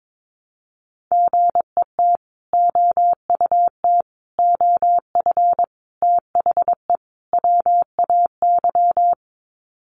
Morse code, heard as a keyed tone, about 22 wpm, 700 Hz. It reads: GET OUT OF THE WAY
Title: Sets of 5 Words Encore 22wpm